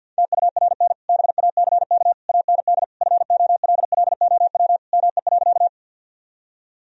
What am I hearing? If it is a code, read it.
Morse code, heard as a keyed tone, about 52 wpm, 700 Hz. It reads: TURN BACK AND FOLLOW ME.
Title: Sentences from Top 1000 Words - Review Part 1 - 50wpm